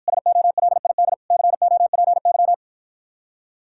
Morse code, heard as a keyed tone, about 52 wpm, 700 Hz. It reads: SOLID COPY